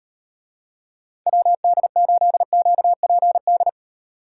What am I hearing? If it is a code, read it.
Morse code, heard as a keyed tone, about 38 wpm, 700 Hz. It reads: WD8QPD